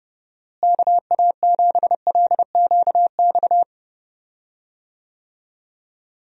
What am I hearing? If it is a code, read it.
Morse code, heard as a keyed tone, about 30 wpm, 700 Hz. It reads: KA7LQX
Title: USA Call Signs - 6 Characters - Repeated 2x - 30wpm